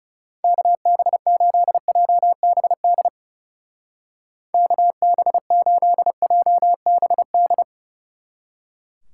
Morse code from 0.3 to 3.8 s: KB8JBD

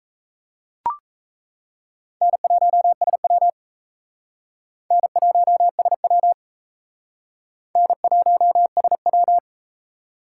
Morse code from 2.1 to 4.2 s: N1SW

